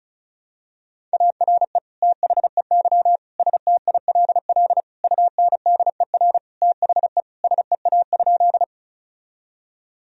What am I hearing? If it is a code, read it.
Morse code, heard as a keyed tone, about 36 wpm, 700 Hz. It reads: ARE THEY STILL UNDER THE SEA?
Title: Sentences from Top 1000 Words - Review Part 1 - 35wpm